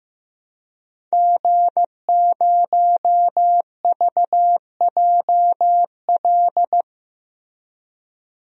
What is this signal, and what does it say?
Morse code, heard as a keyed tone, about 15 wpm, 700 Hz. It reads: G0VJL
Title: DXCC-100 Call Signs - 5 Characters - Repeated 3x - 15wpm